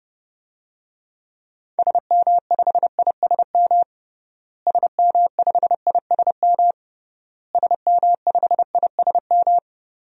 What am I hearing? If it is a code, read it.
Morse code, heard as a keyed tone, about 32 wpm, 700 Hz. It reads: SM5ISM SM5ISM SM5ISM
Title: DXCC-100 Call Signs - 6 Characters - Repeated 3x - 30wpm